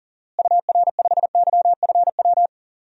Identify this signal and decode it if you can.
Morse code, heard as a keyed tone, about 37 wpm, 700 Hz. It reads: UR5YFW
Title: DXCC-100 Call Signs - 6 Characters - Repeated 2x - 40wpm